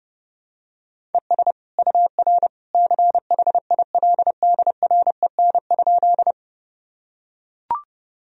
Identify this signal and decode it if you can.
Morse code, heard as a keyed tone, about 30 wpm, 700 Hz. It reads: ES UR CHILDREN?
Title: Sentences from Top 300 Words and Abbreviations 30wpm